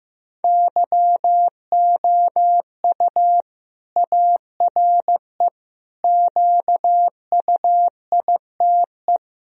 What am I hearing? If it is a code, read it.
Morse code, heard as a keyed tone, about 16 wpm, 700 Hz. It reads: YOU ARE QUITE
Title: Sentences from Top 1000 Words - Review Part 2 - 15wpm